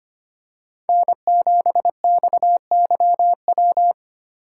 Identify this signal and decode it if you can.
Morse code, heard as a keyed tone, about 29 wpm, 700 Hz. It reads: N7XYW
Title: Speed Racing - USA 1x3 Call Signs 25wpm